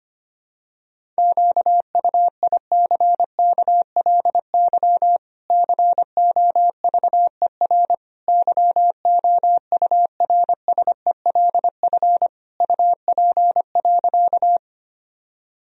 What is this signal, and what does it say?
Morse code, heard as a keyed tone, about 25 wpm, 700 Hz. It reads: QUICKLY COVER YOURSELF UP.
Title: Sentences from Top 1000 Words - Review Part 1 - 25wpm